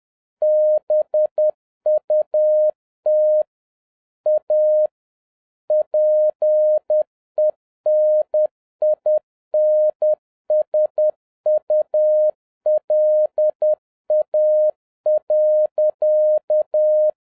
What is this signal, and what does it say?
Morse code, heard as a keyed tone, about 10 wpm, 610 Hz. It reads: BUT A PENINSULA.
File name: 2011-07-mp3 / n31 THE 3.11 2 10wpm